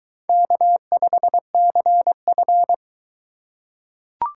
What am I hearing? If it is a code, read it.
Morse code, heard as a keyed tone, about 20 wpm, 700 Hz. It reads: K5CF